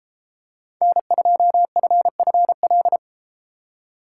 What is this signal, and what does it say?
Morse code, heard as a keyed tone, about 30 wpm, 700 Hz. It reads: N2FFL